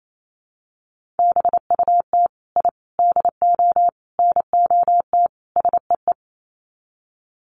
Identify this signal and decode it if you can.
Morse code, heard as a keyed tone, about 28 wpm, 700 Hz. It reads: BUT I DO NOT SEE
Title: Sets of 5 Words Encore 28wpm